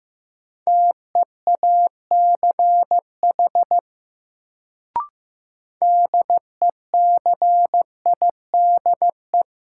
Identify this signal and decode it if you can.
Morse code, heard as a keyed tone, about 15 wpm, 700 Hz. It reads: TEACH DECIDE
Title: Sentences from Top 500 Words - New Word Drill - 15wpm